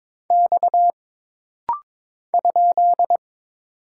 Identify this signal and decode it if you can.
Morse code, heard as a keyed tone, about 22 wpm, 700 Hz. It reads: X ?
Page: This rapid-fire timing is geared to push your ICR (Instant Character Recognition) skills.